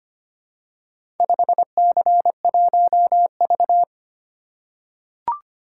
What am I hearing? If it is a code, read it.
Morse code, heard as a keyed tone, about 25 wpm, 700 Hz. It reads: 5C1V
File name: practice / 25wpm / Four Letter-Number Combinations - 25wpm